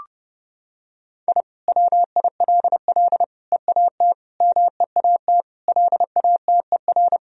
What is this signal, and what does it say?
Morse code, heard as a keyed tone, about 29 wpm, 700 Hz. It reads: I WILL EAT MEAT LATER
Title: Sentences from Top 1000 Words - Review Part 3 - 30wpm